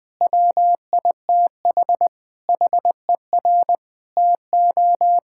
Morse code, heard as a keyed tone, about 20 wpm, 700 Hz. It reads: WITH HER TO
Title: Sentences from Top 1000 Words - Review Part 5 - 20wpm